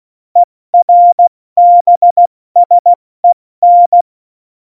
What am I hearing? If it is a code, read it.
Morse code, heard as a keyed tone, about 16 wpm, spaced as for 14 wpm, 700 Hz. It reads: ERBSEN